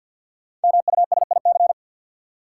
Morse code, heard as a keyed tone, about 52 wpm, 700 Hz. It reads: MUSIC
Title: Sentences from Top 300 Words - New Word Drill - 50wpm